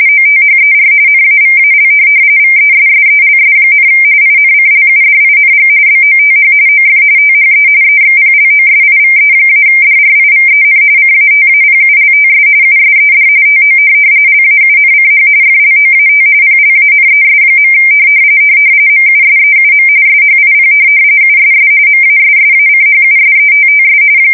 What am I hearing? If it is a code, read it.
RTTY, about 45.45 baud. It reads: OTHER NAVAL VESSELS, INCLUDING BOTH OF THE EXPLOSIVE-DRIVEN CATAPULTS USED TO LAUNCH FLOAT PLANES. MUSEUM SHIP USS MASSACHUSETTS
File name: fsk170 / USS-Massachusetts